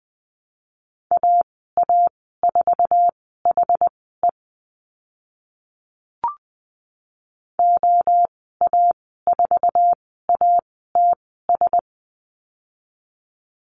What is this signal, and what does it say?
Morse code, heard as a keyed tone, about 20 wpm, spaced as for 14 wpm, 700 Hz. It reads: AA4HE OA4ATS
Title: 062 Lesson 11g - Letter h - Call Signs 20wpm